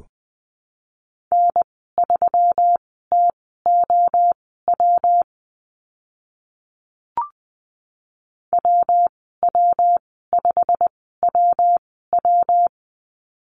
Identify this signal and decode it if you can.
Morse code, heard as a keyed tone, about 20 wpm, spaced as for 14 wpm, 700 Hz. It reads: N3TOW WW5WW
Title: 122 Lesson 20g - Number 3 - Call Signs 20wpm